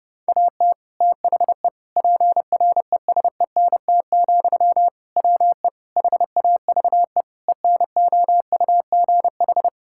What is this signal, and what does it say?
Morse code, heard as a keyed tone, about 32 wpm, 700 Hz. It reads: AT THE PRESENT, WE HAVE ENOUGH
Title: Sentences from Top 1000 Words - Review Part 5 - 30wpm